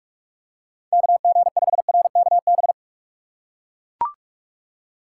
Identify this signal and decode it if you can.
Morse code, heard as a keyed tone, about 41 wpm, 700 Hz. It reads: KG5RKB
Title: USA Call Signs - 6 Characters - 2x Character Spacing - 45wpm